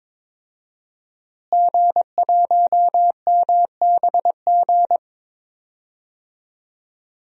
Morse code, heard as a keyed tone, about 22 wpm, 700 Hz. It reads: G1MBG
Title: DXCC-100 Call Signs - 5 Characters - Repeated 3x - 22wpm